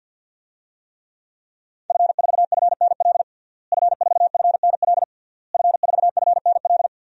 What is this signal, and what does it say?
Morse code, heard as a keyed tone, about 50 wpm, 700 Hz. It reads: F4FNL F4FNL F4FNL
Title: DXCC-100 Call Signs - 5 Characters - Repeated 3x - 50wpm